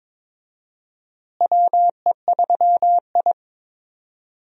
Morse code, heard as a keyed tone, about 22 wpm, 700 Hz. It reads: WE3I
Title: USA Call Signs - 4 Characters - Repeated 3x - 22wpm